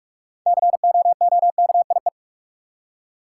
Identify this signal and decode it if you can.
Morse code, heard as a keyed tone, about 46 wpm, 700 Hz. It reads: COOKIE
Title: Six Letter Words - 45wpm